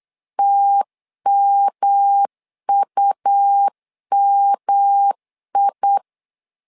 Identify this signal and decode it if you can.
Morse code, heard as a keyed tone, about 8 wpm, 800 Hz. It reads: TMUMI
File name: LEZIONI / 3 MWRO / 3 M 8wpm 2300